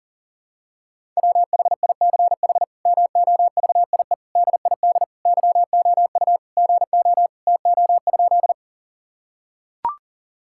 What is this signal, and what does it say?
Morse code, heard as a keyed tone, about 40 wpm, 700 Hz. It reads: WHICH MOVIE DID YOU GO TO?
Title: Sentences from Top 1000 Words - Review Part 5 - 40wpm